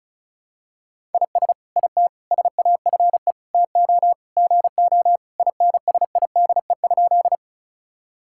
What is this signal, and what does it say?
Morse code, heard as a keyed tone, about 34 wpm, 700 Hz. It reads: IS IT SAFE TO GO INSIDE?
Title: Sentences from Top 1000 Words - Review Part 6 - 35wpm